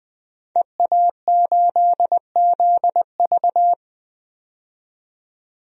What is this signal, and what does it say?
Morse code, heard as a keyed tone, about 20 wpm, 700 Hz. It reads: EA8ZV
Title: DXCC-100 Call Signs - 5 Characters - 20wpm